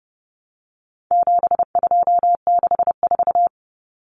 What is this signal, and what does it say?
Morse code, heard as a keyed tone, about 30 wpm, 700 Hz. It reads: 7264
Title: Numbers - 4-Digit 30wpm